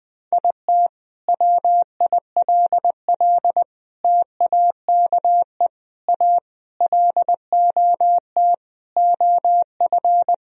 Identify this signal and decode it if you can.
Morse code, heard as a keyed tone, about 20 wpm, 700 Hz. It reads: IT WILL TAKE A LOT OF
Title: Sentences from Top 1000 Words - Review Part 4 - 20wpm